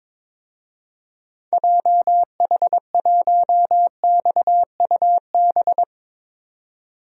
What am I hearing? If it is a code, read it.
Morse code, heard as a keyed tone, about 22 wpm, 700 Hz. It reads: JH1XUB
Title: DXCC-100 Call Signs - 6 Characters - 22wpm